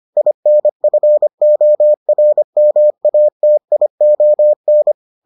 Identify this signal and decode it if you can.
Morse code, heard as a keyed tone, about 25 wpm, 600 Hz. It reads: INFORMATION